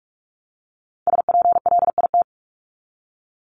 Morse code, heard as a keyed tone, about 44 wpm, 700 Hz. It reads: SPLIT
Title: Top 1751-2000 Words 45wpm